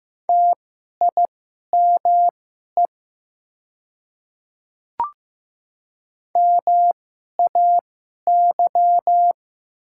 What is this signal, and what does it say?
Morse code, heard as a keyed tone, about 15 wpm, spaced as for 11 wpm, 700 Hz. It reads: TIME MAY
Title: 2x Character Spacing - Top 100 Words - 15wpm